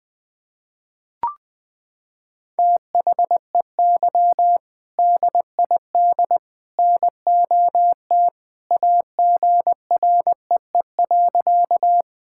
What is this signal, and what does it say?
Morse code, heard as a keyed tone, about 20 wpm, 700 Hz. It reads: THEY DID NOT AGREE.
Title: Sentences with 3-Word Phrases - 20wpm